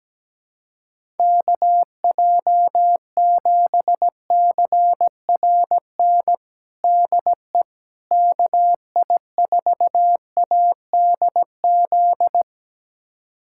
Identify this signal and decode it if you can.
Morse code, heard as a keyed tone, about 17 wpm, 700 Hz. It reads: KJ7CRN DE KI4ADZ
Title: USA Callsign DE Callsign - 6 Characters - 17wpm